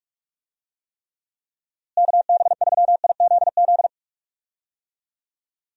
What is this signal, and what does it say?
Morse code, heard as a keyed tone, about 46 wpm, 700 Hz. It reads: KB3IZZ
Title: USA Call Signs - 6 Characters - Repeated 2x - 45wpm